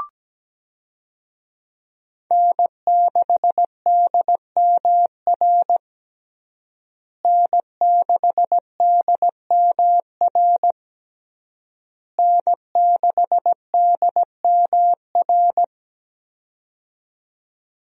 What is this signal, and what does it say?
Morse code, heard as a keyed tone, about 17 wpm, 700 Hz. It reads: N6DMR N6DMR N6DMR
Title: USA Call Signs - 5 Characters - Repeated 3x - 17wpm